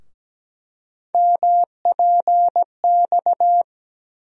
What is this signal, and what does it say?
Morse code, heard as a keyed tone, about 17 wpm, 700 Hz. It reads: MPX
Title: USA Call Sign Postfixes - 3 Letters 17wpm